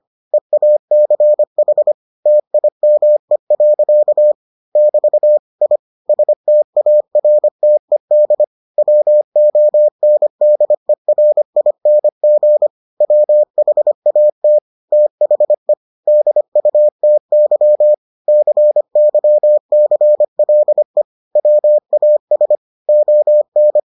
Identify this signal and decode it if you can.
Morse code, heard as a keyed tone, about 25 wpm, 600 Hz. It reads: EACH TIME. <BT> I STARTED WONDERING WHAT THE DUTY CYCLE WAS ON